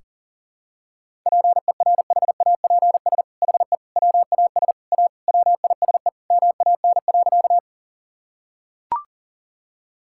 Morse code, heard as a keyed tone, about 40 wpm, 700 Hz. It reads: PERHAPS HE WAS A WISE MAN.
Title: Sentences from Top 1600 Words 40wpm